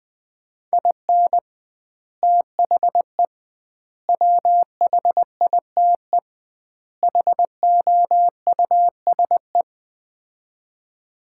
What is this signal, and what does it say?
Morse code, heard as a keyed tone, about 19 wpm, 700 Hz. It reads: IN THE WHITE HOUSE
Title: 2x Word-Spacing - Sets of 4 Words - 20wpm